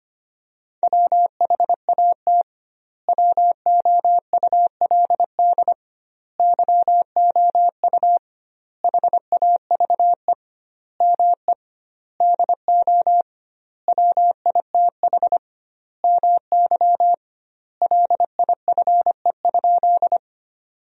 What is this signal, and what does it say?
Morse code, heard as a keyed tone, about 25 wpm, 700 Hz. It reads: WHAT WOULD YOU HAVE ME DO WITH MY LIFE?
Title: 2x Word-Spacing - Sentences from Top 300 Words - 25wpm